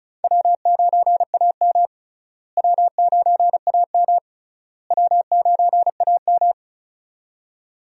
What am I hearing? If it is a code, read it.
Morse code, heard as a keyed tone, about 36 wpm, 700 Hz. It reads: W9AM W9AM W9AM
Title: USA Call Signs - 4 Characters - Repeated 3x - 35wpm